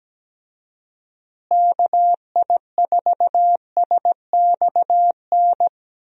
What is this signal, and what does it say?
Morse code, heard as a keyed tone, about 17 wpm, 700 Hz. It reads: KI4SXN
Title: USA Call Signs - 6 Characters - Repeated 3x - 17wpm